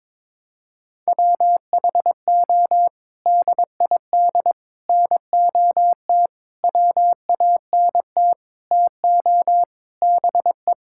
Morse code, heard as a keyed tone, about 22 wpm, 700 Hz. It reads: WHO DID NOT WANT TO BE
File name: practice / 22wpm / Sets of 7 Words - 22wpm